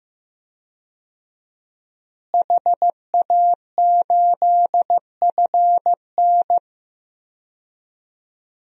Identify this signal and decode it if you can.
Morse code, heard as a keyed tone, about 16 wpm, 700 Hz. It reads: HA8FN
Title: DXCC-100 Call Signs - 5 Characters - Repeated 3x - 15wpm